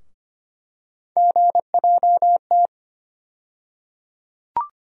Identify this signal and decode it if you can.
Morse code, heard as a keyed tone, about 25 wpm, 700 Hz. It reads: GJT
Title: USA Call Sign Postfixes - 3 Letters 25wpm